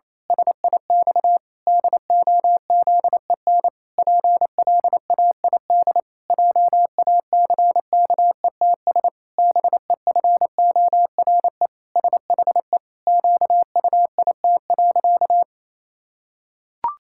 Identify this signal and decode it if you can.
Morse code, heard as a keyed tone, about 28 wpm, 700 Hz. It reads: SIX DOZEN PLAID JACKETS BEFORE SHE QUIT.
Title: Pangram Challenge v2 - 28wpm